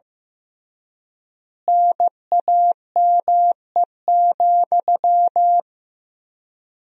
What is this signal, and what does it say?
Morse code, heard as a keyed tone, about 15 wpm, 700 Hz. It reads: NAME,
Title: 3x Word-Spacing - Sentences from Top 500 Words - 15wpm